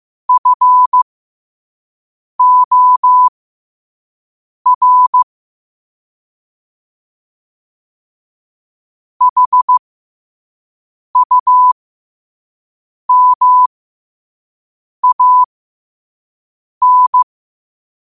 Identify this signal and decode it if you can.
Morse code, heard as a keyed tone, about 15 wpm, spaced as for 5 wpm, 1000 Hz. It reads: FOR HUMAN
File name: qotdfiles / 5wpm